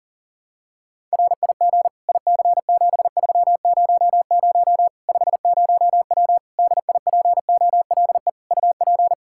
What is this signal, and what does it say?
Morse code, heard as a keyed tone, about 40 wpm, 700 Hz. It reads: RIG IC7300 50W DIPOLE UP
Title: Practice QSO - Exchange 2 - Info - 40wpm